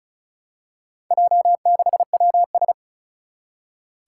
Morse code, heard as a keyed tone, about 36 wpm, 700 Hz. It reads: J6WS
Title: Four Challenging Letter-Number Combinations 35wpm